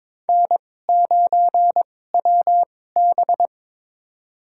Morse code, heard as a keyed tone, about 23 wpm, spaced as for 16 wpm, 700 Hz. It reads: N9WB
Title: USA Call Signs - 4 Characters - 2x Character Spacing - 22wpm